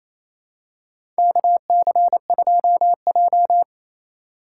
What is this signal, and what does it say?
Morse code, heard as a keyed tone, about 27 wpm, 700 Hz. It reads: KC2J
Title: USA Call Signs - 4 Characters - 6x Character Spacing - 28wpm